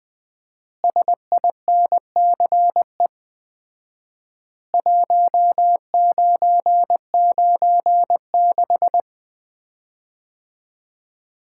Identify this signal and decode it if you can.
Morse code, heard as a keyed tone, about 20 wpm, 700 Hz. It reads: SINCE 1996
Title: Common QSO Phrases - 4x Word-Spacing - 20wpm